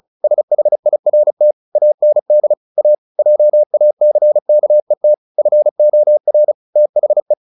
Morse code, heard as a keyed tone, about 34 wpm, 600 Hz. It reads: SHIRT AND A JACKET FOR THE
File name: the old man and the sea / the old man and the sea 1 35wpm